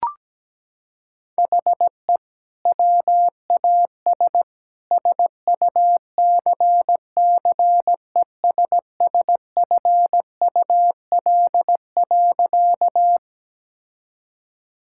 Morse code, heard as a keyed tone, about 17 wpm, 700 Hz. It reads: HE WAS SUCCESSFUL.
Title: Sentences from Top 1000 Words - Review Part 6 - 17wpm